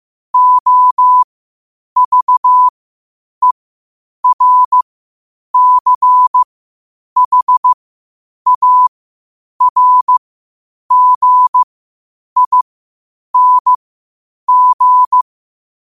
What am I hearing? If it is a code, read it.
Morse code, heard as a keyed tone, about 15 wpm, spaced as for 8 wpm, 1000 Hz. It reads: OVERCHARGING
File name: qotdfiles / 8wpm